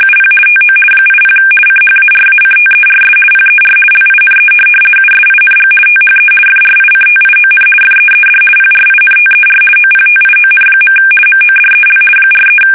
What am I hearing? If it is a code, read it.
RTTY, about 50 baud. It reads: UP TO 80 CHARACTERS PER LINE USING THE FRICTION FEED PRINTER OPTION. MODEL 43
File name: fsk850-50B / teletype-corp